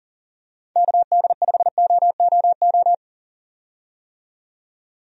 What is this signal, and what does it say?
Morse code, heard as a keyed tone, about 40 wpm, 700 Hz. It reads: KD5OOO
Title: USA Call Signs - 6 Characters - 40wpm